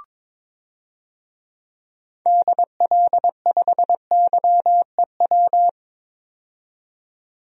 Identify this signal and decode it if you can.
Morse code, heard as a keyed tone, about 21 wpm, 700 Hz. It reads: DL5YEW
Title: DXCC-100 Call Signs - 6 Characters - 22wpm